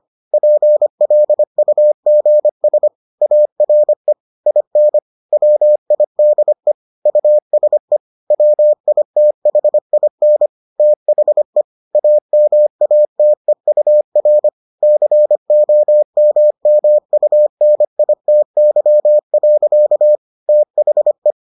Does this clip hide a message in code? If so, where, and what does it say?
Yes: Morse code from 0.3 to 21.3 s, heard as a keyed tone, about 25 wpm, 600 Hz. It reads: PLUGS ARE IN WIDE USE WITHIN THE AMATEUR COMMUNITY. THE